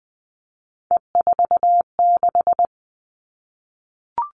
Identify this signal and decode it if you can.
Morse code, heard as a keyed tone, about 20 wpm, 700 Hz. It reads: E46
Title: All Cut Numbers - 3-Digit 20wpm